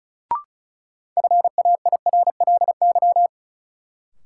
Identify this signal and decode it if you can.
Morse code, heard as a keyed tone, about 34 wpm, 700 Hz. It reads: FAIRLY
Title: Sentences from Top 1100 Words - New Word Drill - 35wpm